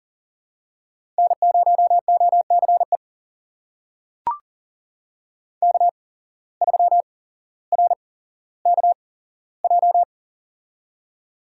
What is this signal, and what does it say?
Morse code, heard as a keyed tone, about 40 wpm, spaced as for 17 wpm, 700 Hz. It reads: N0OCE K3RKJ
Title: USA Call Signs - 5 Characters - 8x Character Spacing - 40wpm